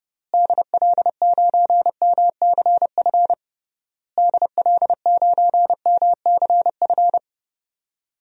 Morse code, heard as a keyed tone, about 30 wpm, 700 Hz. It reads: DL9MCF DL9MCF
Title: DXCC-100 Call Signs - 6 Characters - Repeated 3x - 30wpm